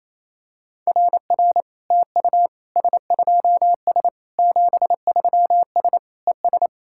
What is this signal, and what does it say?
Morse code, heard as a keyed tone, about 27 wpm, 690 Hz. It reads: RR TU S2S 73S ES
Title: SOTA QSO Practice 28wpm